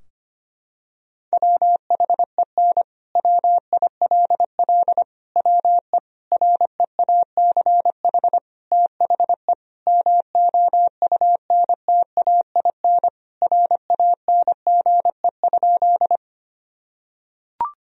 Morse code, heard as a keyed tone, about 25 wpm, 700 Hz. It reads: WHEN WILL WE REACH THE MOUNTAIN RANGE?
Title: Sentences from Top 1000 Words - Review Part 4 - 25wpm